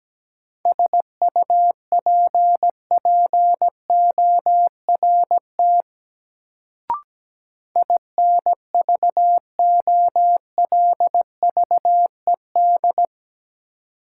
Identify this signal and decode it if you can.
Morse code, heard as a keyed tone, about 17 wpm, 700 Hz. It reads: SUPPORT INVOLVED
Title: Sentences from Top 800 Words - New Word Drill - 17wpm